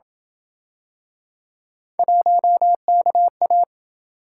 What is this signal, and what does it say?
Morse code, heard as a keyed tone, about 27 wpm, 700 Hz. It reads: 1KA